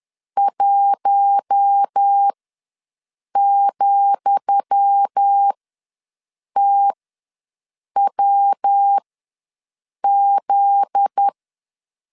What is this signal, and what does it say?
Morse code, heard as a keyed tone, about 10 wpm, spaced as for 6 wpm, 800 Hz. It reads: 1,TWZ
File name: LEZIONI / 10 SIGNS / 10D COMA ws 10wpm 4314